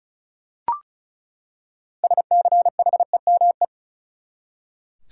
Morse code, heard as a keyed tone, about 35 wpm, 700 Hz. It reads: SCHEME
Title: Six Letter Words - 35wpm